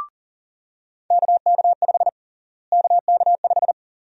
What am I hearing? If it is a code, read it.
Morse code, heard as a keyed tone, about 40 wpm, 700 Hz. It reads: KK5 KK5
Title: USA Call Sign Prefixes - Repeated 2x - 40wpm